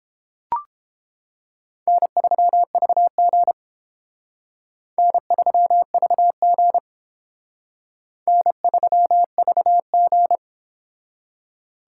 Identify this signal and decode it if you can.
Morse code, heard as a keyed tone, about 28 wpm, 700 Hz. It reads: N3VG N3VG N3VG